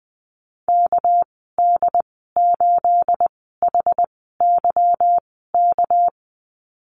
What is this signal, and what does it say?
Morse code, heard as a keyed tone, about 20 wpm, spaced as for 15 wpm, 700 Hz. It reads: KD8HYK
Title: 231 Lesson 37g - Letter x - Call Signs 20wpm